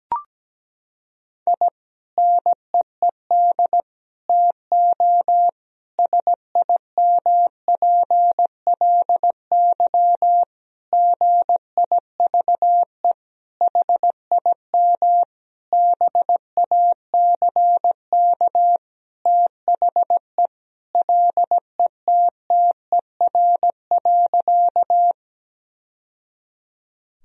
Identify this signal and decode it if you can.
Morse code, heard as a keyed tone, about 17 wpm, 700 Hz. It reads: I NEED TO SIMPLY GIVE HIM BACK THE LETTER.
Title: Sentences from Top 1000 Words - Review Part 4 - 17wpm